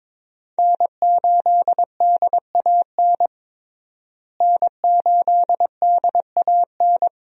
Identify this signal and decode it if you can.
Morse code, heard as a keyed tone, about 22 wpm, 700 Hz. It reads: N8DAN N8DAN
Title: USA Call Signs - 5 Characters - Repeated 2x - 22wpm